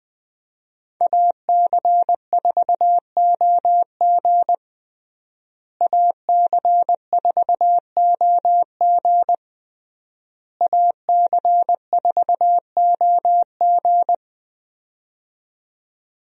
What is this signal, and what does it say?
Morse code, heard as a keyed tone, about 20 wpm, 700 Hz. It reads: AC4OG AC4OG AC4OG
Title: USA Call Signs - 5 Characters - Repeated 3x - 20wpm